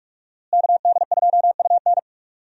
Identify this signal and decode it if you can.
Morse code, heard as a keyed tone, about 46 wpm, 700 Hz. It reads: KD2UN
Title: USA Call Signs - 5 Characters - Repeated 3x - 45wpm